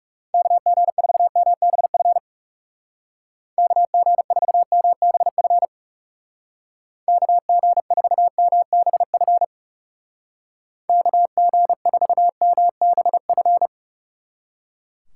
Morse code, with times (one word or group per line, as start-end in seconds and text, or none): 0.3-2.2 KG4MBF
3.6-5.7 KG4MBF
7.1-9.4 KG4MBF
10.9-13.7 KG4MBF